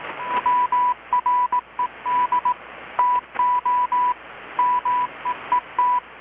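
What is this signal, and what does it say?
Morse code, heard as a keyed tone, about 18 wpm, 990 Hz. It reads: WRED TO MEET
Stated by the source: Morse from KPH's Night of Nights XX maritime radiotelegraphy event, received in Chiapas, Mexico about 3,800 km away on a Kenwood R-600 with a grounded 10 m long-wire antenna.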